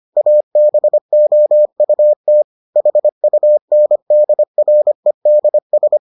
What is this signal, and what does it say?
Morse code, heard as a keyed tone, about 25 wpm, 600 Hz. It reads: ABOUT HUNDREDS